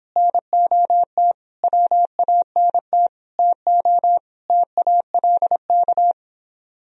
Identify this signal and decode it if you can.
Morse code, heard as a keyed tone, about 26 wpm, 700 Hz. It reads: NOT WANT TO TALK